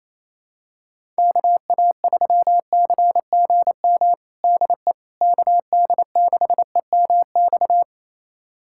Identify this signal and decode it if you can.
Morse code, heard as a keyed tone, about 28 wpm, 700 Hz. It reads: KA3CGM DE KD6EMX
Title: USA Callsign DE Callsign 28wpm